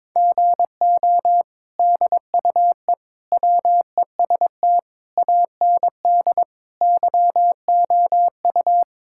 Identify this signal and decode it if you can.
Morse code, heard as a keyed tone, about 22 wpm, 700 Hz. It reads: GO DUE WEST AND YOU
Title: Sentences from Top 1000 Words - Review Part 6 - 22wpm